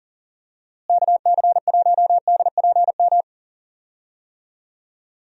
Morse code, heard as a keyed tone, about 40 wpm, 700 Hz. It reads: KC1DPM
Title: USA Call Signs - 6 Characters - Repeated 3x - 40wpm